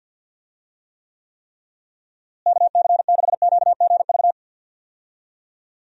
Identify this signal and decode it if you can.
Morse code, heard as a keyed tone, about 51 wpm, 700 Hz. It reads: KC6QGV
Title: USA Call Signs - 6 Characters - Repeated 3x - 50wpm